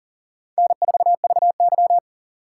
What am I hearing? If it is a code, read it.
Morse code, heard as a keyed tone, about 40 wpm, 700 Hz. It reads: N4VY
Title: USA Call Signs - 4 Characters - 2x Character Spacing - 40wpm